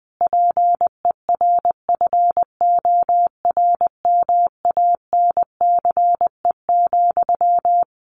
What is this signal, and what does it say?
Morse code, heard as a keyed tone, about 20 wpm, 700 Hz. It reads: PERFORMANCE,
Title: Sentences from Top 1500 Words 20wpm